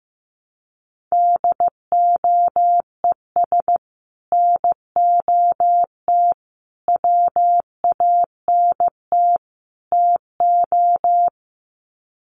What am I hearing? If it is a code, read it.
Morse code, heard as a keyed tone, about 15 wpm, 700 Hz. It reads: DOES NOT WANT TO
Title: Sets of 4 Words 15wpm